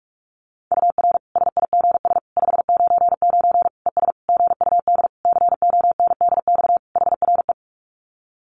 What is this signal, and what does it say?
Morse code, heard as a keyed tone, about 44 wpm, 700 Hz. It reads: UR SIGS 599 ES GUD CONDX HRE
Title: Common QSO Phrases 45wpm